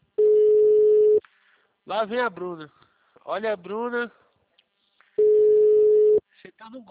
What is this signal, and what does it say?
Ringback tone (Europe, 425 Hz), with speech.
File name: out- -2007- - - .1365